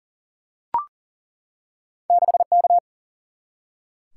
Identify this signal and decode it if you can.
Morse code, heard as a keyed tone, about 40 wpm, 700 Hz. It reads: 6K
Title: Single-Letter Single-Number 40wpm